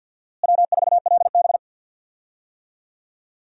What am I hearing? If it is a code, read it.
Morse code, heard as a keyed tone, about 50 wpm, 700 Hz. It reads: W4LB
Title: USA Call Signs - 4 Characters - Repeated 2x - 50wpm